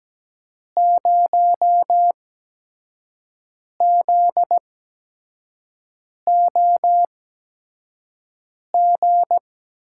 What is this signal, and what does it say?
Morse code, heard as a keyed tone, about 17 wpm, spaced as for 5 wpm, 700 Hz. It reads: 0ZOG
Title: USA Call Signs - 5 Characters - 8x Character Spacing - 17wpm